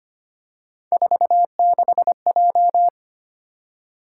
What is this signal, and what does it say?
Morse code, heard as a keyed tone, about 25 wpm, 700 Hz. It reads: 46J
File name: Three Letter-Number Combinations - 25wpm